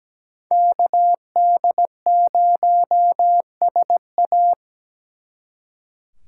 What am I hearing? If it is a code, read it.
Morse code, heard as a keyed tone, about 20 wpm, 700 Hz. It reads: KD0SA
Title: Speed Racing - USA 2x2 Call Signs 17wpm